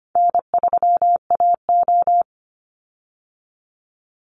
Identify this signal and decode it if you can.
Morse code, heard as a keyed tone, about 25 wpm, 700 Hz. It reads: N3AO
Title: Easy Call Signs Repeated 3x - 25wpm